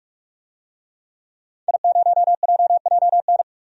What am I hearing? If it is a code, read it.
Morse code, heard as a keyed tone, about 45 wpm, 700 Hz. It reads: I0JJN
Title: DXCC-100 Call Signs - 5 Characters - 45wpm